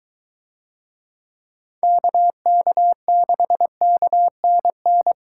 Morse code, heard as a keyed tone, about 21 wpm, 700 Hz. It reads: KK6KNN